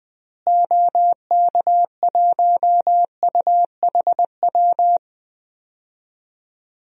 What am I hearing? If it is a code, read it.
Morse code, heard as a keyed tone, about 21 wpm, 700 Hz. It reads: OK1UHW
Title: DXCC-100 Call Signs - 6 Characters - Repeated 3x - 20wpm